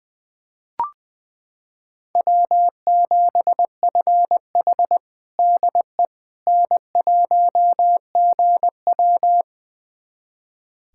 Morse code, heard as a keyed tone, about 20 wpm, 700 Hz. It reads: W7FH DE N1GW
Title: USA Callsign DE Callsign - 4 Characters - 20wpm